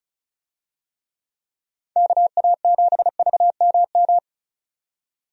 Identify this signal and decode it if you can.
Morse code, heard as a keyed tone, about 34 wpm, 700 Hz. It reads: KA7VMM
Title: USA Call Signs - 6 Characters - 35wpm